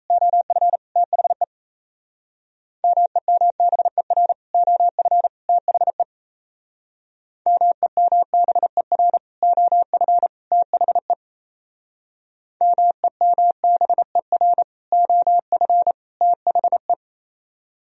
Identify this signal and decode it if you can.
Morse code, heard as a keyed tone, about 32 wpm, 700 Hz. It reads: OF THE MEMBER OF THE MEMBER OF THE MEMBER OF THE
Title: Speed-Racing - Sets of 3 Words 28wpm